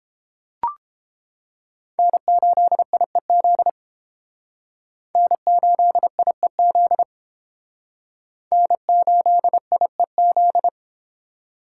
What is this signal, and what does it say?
Morse code, heard as a keyed tone, about 32 wpm, 700 Hz. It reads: N8IEZ N8IEZ N8IEZ